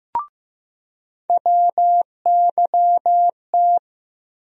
Morse code, heard as a keyed tone, about 15 wpm, 700 Hz. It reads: WYT
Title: USA Call Sign Postfixes - 3 Letters 15wpm